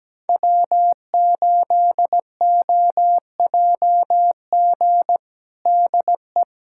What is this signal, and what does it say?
Morse code, heard as a keyed tone, about 17 wpm, 700 Hz. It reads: W8OJG DE
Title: USA Callsign DE Callsign - 5 Characters - 17wpm